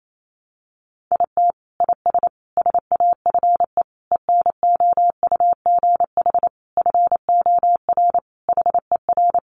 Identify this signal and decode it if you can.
Morse code, heard as a keyed tone, about 28 wpm, 700 Hz. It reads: IT IS SAFE ENOUGH FOR HER
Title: Sentences from Top 900 Words 28wpm